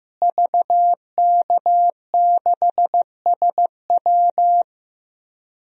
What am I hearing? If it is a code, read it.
Morse code, heard as a keyed tone, about 15 wpm, 700 Hz. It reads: VK6SW
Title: DXCC-100 Call Signs - 5 Characters - Repeated 3x - 15wpm